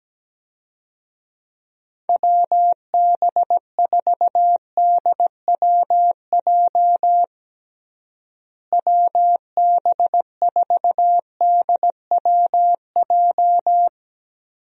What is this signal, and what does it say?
Morse code, heard as a keyed tone, about 17 wpm, 700 Hz. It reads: WB4DWJ WB4DWJ
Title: USA Call Signs - 6 Characters - Repeated 2x - 17wpm